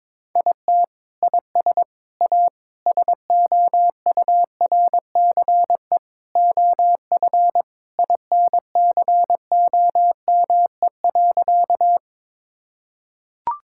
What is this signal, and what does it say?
Morse code, heard as a keyed tone, about 22 wpm, 700 Hz. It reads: IT IS A SOURCE OF INCOME.
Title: Sentences from Top 1000 Words - Review Part 5 - 22wpm